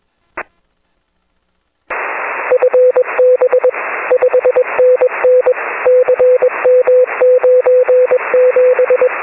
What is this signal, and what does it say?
Morse code, heard as a keyed tone, about 21 wpm, 500 Hz, with radio noise. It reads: FB 5NN CM97
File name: 6m-n6ml